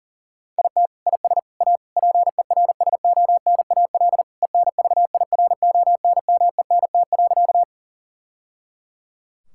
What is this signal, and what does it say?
Morse code, heard as a keyed tone, about 40 wpm, 700 Hz. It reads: IT IS A PERSONAL ENVIRONMENT.
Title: Sentences from Top 1000 Words - Review Part 5 - 40wpm